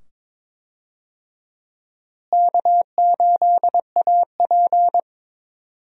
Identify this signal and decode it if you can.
Morse code, heard as a keyed tone, about 22 wpm, 700 Hz. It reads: K8AP
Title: USA Call Signs - 4 Characters - Repeated 2x - 22wpm